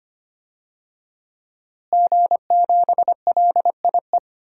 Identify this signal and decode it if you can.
Morse code, heard as a keyed tone, about 25 wpm, 700 Hz. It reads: G7LIE
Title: DXCC-100 Call Signs - 5 Characters - Repeated 2x - 25wpm